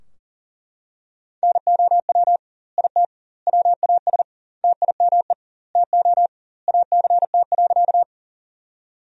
Morse code, heard as a keyed tone, about 40 wpm, 700 Hz. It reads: NOW IT WAS TIME TO ACT.
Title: 2x Word-Spacing - Sentences from Top 200 Words - 40wpm